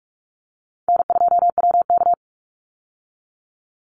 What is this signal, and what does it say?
Morse code, heard as a keyed tone, about 44 wpm, 700 Hz. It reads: N2WK
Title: Easy Call Signs Repeated 2x - 45wpm